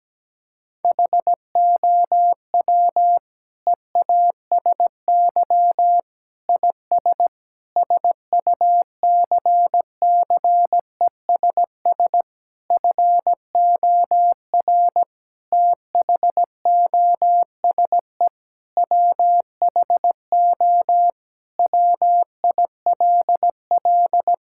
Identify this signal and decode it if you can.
Morse code, heard as a keyed tone, about 17 wpm, 700 Hz. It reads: HOW EASY IS SUCCESS FOR THOSE WHO WILL
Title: Sentences from Top 1000 Words - Review Part 6 - 17wpm